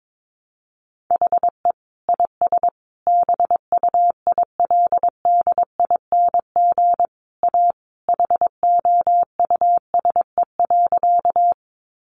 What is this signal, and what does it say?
Morse code, heard as a keyed tone, about 22 wpm, 700 Hz. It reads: HE IS BUILDING A HOUSE.
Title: Sentences from Top 700 Words 22wpm